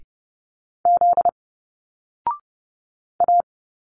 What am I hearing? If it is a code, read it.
Morse code, heard as a keyed tone, about 30 wpm, 700 Hz. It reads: Z A
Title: Single Letters 30wpm